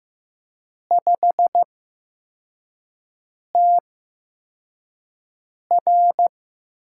Morse code, heard as a keyed tone, about 15 wpm, spaced as for 4 wpm, 700 Hz. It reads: 5TR
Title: USA Call Signs - 5 Characters - 8x Character Spacing - 15wpm